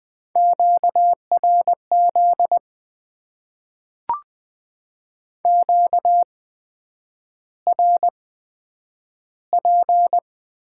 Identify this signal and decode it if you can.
Morse code, heard as a keyed tone, about 20 wpm, spaced as for 9 wpm, 700 Hz. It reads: QRZ QRP
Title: Top 90 QSO Elements - 8x Character Spacing - 20wpm